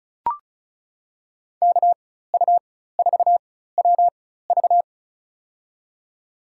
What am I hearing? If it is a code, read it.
Morse code, heard as a keyed tone, about 36 wpm, spaced as for 16 wpm, 700 Hz. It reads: KU4WV
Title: USA Call Signs - 5 Characters - 4x Character Spacing - 35wpm